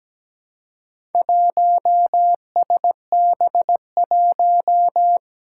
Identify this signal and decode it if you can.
Morse code, heard as a keyed tone, about 17 wpm, 700 Hz. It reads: 1SB1
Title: Four Challenging Letter-Number Combinations 17wpm